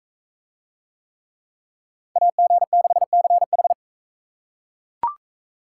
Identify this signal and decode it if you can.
Morse code, heard as a keyed tone, about 40 wpm, 700 Hz. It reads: AG6CH